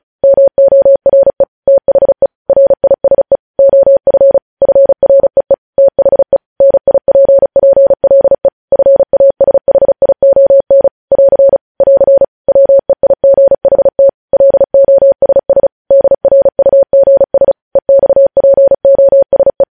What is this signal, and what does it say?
Morse code, heard as a keyed tone, about 36 wpm, 550 Hz. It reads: MORE THE RISE OF FREE THE NIPPLE FASHION <AR> <AR> WEIGHT LOSS DRUGS EXPOSE
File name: podcast / News headlines-1x-35x35